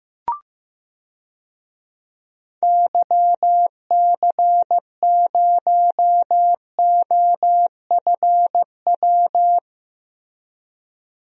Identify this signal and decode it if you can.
Morse code, heard as a keyed tone, about 15 wpm, 700 Hz. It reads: YC0OFW
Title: DXCC-100 Call Signs - 6 Characters - Repeated 3x - 15wpm